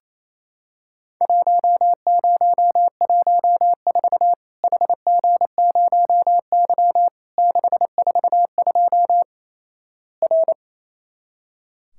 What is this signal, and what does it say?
Morse code, heard as a keyed tone, about 28 wpm, 700 Hz. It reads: 1014 HG0Y 642 R
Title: Worked All Europe DX Contest - QTC Exchange - 28wpm